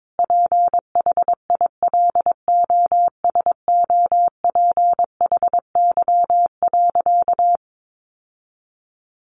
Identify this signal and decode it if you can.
Morse code, heard as a keyed tone, about 22 wpm, 700 Hz. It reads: PHILOSOPHY.
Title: Sentences from Top 900 Words 22wpm